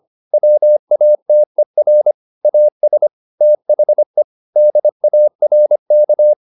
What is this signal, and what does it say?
Morse code, heard as a keyed tone, about 25 wpm, 600 Hz. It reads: WATER AS THE DARK